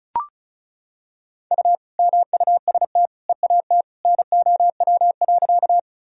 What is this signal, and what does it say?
Morse code, heard as a keyed tone, about 36 wpm, 700 Hz. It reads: U MUST EAT NOW.
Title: Sentences from Top 300 Words and Abbreviations 35wpm